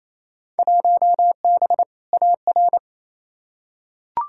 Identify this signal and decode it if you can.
Morse code, heard as a keyed tone, about 28 wpm, 700 Hz. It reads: 1B AR
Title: Field Day Exchanges Encore - 28wpm